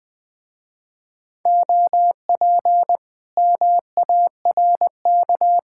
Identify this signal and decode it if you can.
Morse code, heard as a keyed tone, about 20 wpm, 700 Hz. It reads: OP MARK
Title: Speed-Racing Common QSO Phrases - 17wpm